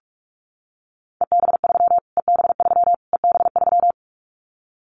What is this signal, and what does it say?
Morse code, heard as a keyed tone, about 44 wpm, 700 Hz. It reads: EB3 EB3 EB3
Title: Callsign Prefixes for Top 25 DXCC - Repeated 3x - 45wpm